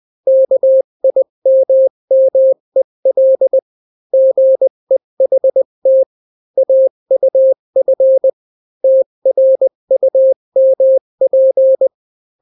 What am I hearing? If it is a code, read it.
Morse code, heard as a keyed tone, about 20 wpm, 530 Hz. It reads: KIMMEL GEHT AUF TRUMP
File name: daily / 18wpm